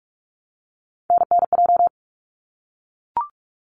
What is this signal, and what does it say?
Morse code, heard as a keyed tone, about 44 wpm, 700 Hz. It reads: NNJ
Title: ARRL Sections 45wpm